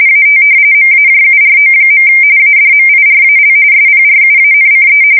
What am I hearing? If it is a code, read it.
RTTY, about 45.45 baud. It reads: AT MAJURO. ADMIRAL SPRUANCE
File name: fsk170 / battleshipnj